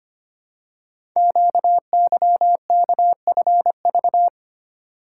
Morse code, heard as a keyed tone, about 25 wpm, 700 Hz. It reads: QYKFV
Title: Code Groups - 5 Letters - 25wpm